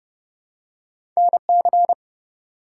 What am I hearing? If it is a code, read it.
Morse code, heard as a keyed tone, about 30 wpm, 700 Hz. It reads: NC
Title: US States and Canadian Provinces 30wpm